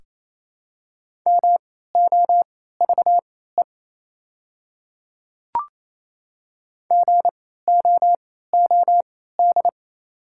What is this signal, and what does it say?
Morse code, heard as a keyed tone, about 28 wpm, spaced as for 16 wpm, 700 Hz. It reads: MOVE GOOD
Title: 3x Character Spacing - Top 200 Words - 28wpm